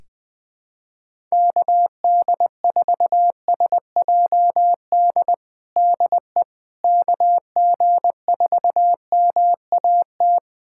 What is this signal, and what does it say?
Morse code, heard as a keyed tone, about 20 wpm, 700 Hz. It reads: KD4SJD DE KG4MAT
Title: USA Callsign DE Callsign 20wpm